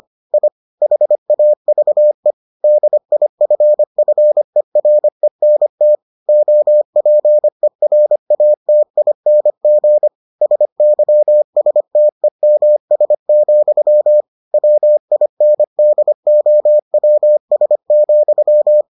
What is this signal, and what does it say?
Morse code, heard as a keyed tone, about 25 wpm, 600 Hz. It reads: I HAVE DIFFERENT OPERATING SYSTEMS, WINDOWS,